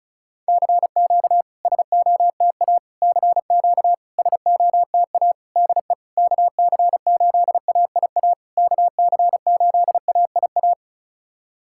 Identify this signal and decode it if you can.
Morse code, heard as a keyed tone, about 36 wpm, 700 Hz. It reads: CQ SOTA CQ SOTA DE KC8AIA KC8AIA
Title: SOTA QSO Practice 35wpm